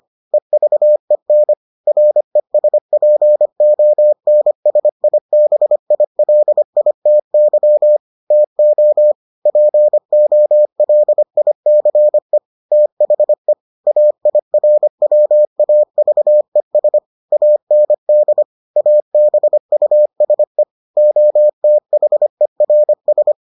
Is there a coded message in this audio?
Yes: Morse code from 0.3 to 23.3 s, heard as a keyed tone, about 25 wpm, 600 Hz. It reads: EVEN RESPONSIBILITY TO POLICE THE AIRWAVES AND ABUSE OTHERS